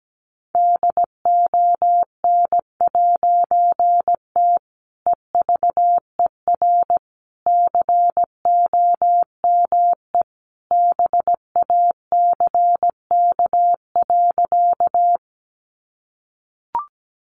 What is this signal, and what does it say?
Morse code, heard as a keyed tone, about 17 wpm, 700 Hz. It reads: DON'T EVER COME BACK.
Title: A Sound Of Thunder 17wpm